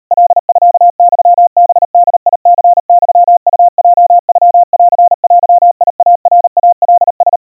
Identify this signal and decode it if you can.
Morse code (Japanese Wabun code), heard as a keyed tone, about 37 wpm, 700 Hz. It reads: カミユハボニユウヲノンデイナイカラ
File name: kawa / Morse sample / kamyu 140 cpm 700 Hz Crispy